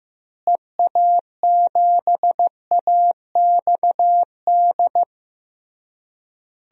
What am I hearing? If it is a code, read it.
Morse code, heard as a keyed tone, about 15 wpm, 700 Hz. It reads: EA7AXD
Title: DXCC-100 Call Signs - 6 Characters - Repeated 3x - 15wpm